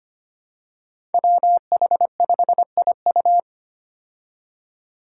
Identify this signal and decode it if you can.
Morse code, heard as a keyed tone, about 25 wpm, 700 Hz. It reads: WH5IU
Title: Five Challenging Letter-Number Combinations 25wpm